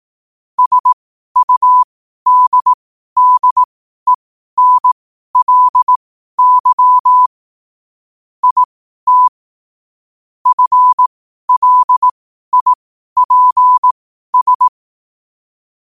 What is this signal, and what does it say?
Morse code, heard as a keyed tone, about 18 wpm, spaced as for 12 wpm, 1000 Hz. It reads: SUDDENLY IT FLIPS
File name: qotdfiles / 12wpm